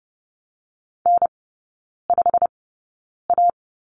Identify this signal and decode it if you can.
Morse code, heard as a keyed tone, about 30 wpm, spaced as for 9 wpm, 700 Hz. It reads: N5A
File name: course / 30:10wpm / 087 Lesson 15e - Number 5 - Three Characters 30wpm